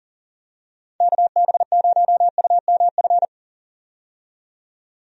Morse code, heard as a keyed tone, about 40 wpm, 700 Hz. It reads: KB0UMF
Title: USA Call Signs - 6 Characters - 40wpm